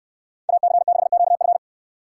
Morse code, heard as a keyed tone, about 69 wpm, 700 Hz. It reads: AC6QF